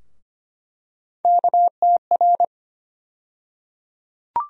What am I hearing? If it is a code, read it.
Morse code, heard as a keyed tone, about 25 wpm, 700 Hz. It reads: KTR
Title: USA Call Sign Postfixes - 3 Letters 25wpm